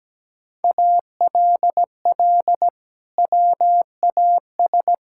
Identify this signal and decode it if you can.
Morse code, heard as a keyed tone, about 17 wpm, 700 Hz. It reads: ALL WAS
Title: Sentences from Top 1000 Words - Review Part 6 - 17wpm